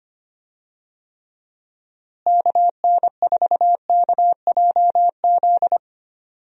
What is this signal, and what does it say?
Morse code, heard as a keyed tone, about 25 wpm, 700 Hz. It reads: KN4KJZ
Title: USA Call Signs - 6 Characters - Repeated 2x - 25wpm